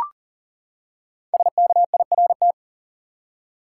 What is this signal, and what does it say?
Morse code, heard as a keyed tone, about 40 wpm, 700 Hz. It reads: SKIRT
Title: Five Letter Words - 40wpm